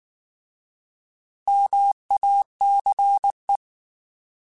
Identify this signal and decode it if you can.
Morse code, heard as a keyed tone, about 19 wpm, 800 Hz. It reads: MACE